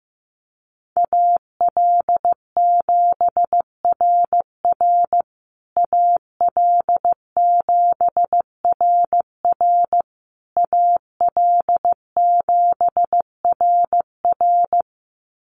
Morse code, heard as a keyed tone, about 15 wpm, 700 Hz. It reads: AL7RR AL7RR AL7RR
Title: USA Call Signs Repeated 3x 15wpm